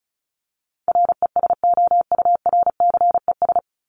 Morse code, heard as a keyed tone, about 34 wpm, 700 Hz. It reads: RESOURCES
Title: ES:S Suffix - Top 100 Words with Base Words 35wpm